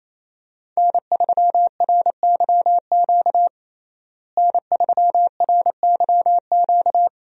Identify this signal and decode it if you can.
Morse code, heard as a keyed tone, about 28 wpm, 700 Hz. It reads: N3RYQ N3RYQ
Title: USA Call Signs - 5 Characters - Repeated 2x - 28wpm